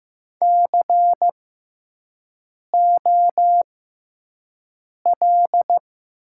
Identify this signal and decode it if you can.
Morse code, heard as a keyed tone, about 15 wpm, spaced as for 5 wpm, 700 Hz. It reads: COL
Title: 6x Character Spacing - Top 300 Words - 15wpm